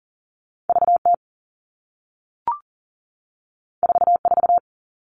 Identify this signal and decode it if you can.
Morse code, heard as a keyed tone, about 40 wpm, 700 Hz. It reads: VT 44
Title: All Cut Numbers - 2-Digit 40wpm